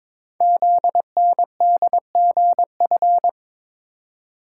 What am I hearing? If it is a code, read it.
Morse code, heard as a keyed tone, about 22 wpm, 700 Hz. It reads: ZNDGF
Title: Code Groups - 5 Letters - 22wpm